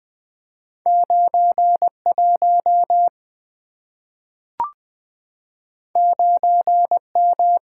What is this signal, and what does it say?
Morse code, heard as a keyed tone, about 20 wpm, 700 Hz. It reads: 91 9M
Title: Two Letter-Number Combinations - 20wpm